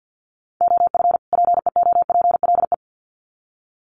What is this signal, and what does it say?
Morse code, heard as a keyed tone, about 50 wpm, 700 Hz. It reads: OF PEOPLE
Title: Sets of 2 Words 50wpm